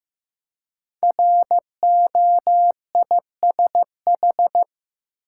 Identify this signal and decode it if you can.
Morse code, heard as a keyed tone, about 15 wpm, 700 Hz. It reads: ROISH
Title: Code Groups - 5 Letters - 15wpm